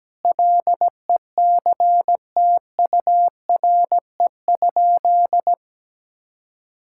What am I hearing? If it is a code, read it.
Morse code, heard as a keyed tone, about 17 wpm, 700 Hz. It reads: LECTURE?
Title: Sentences from Top 1600 Words 17wpm